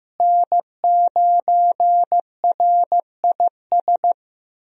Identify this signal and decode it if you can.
Morse code, heard as a keyed tone, about 15 wpm, 700 Hz. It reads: N9RIS
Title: USA Call Signs - 5 Characters - Repeated 2x - 15wpm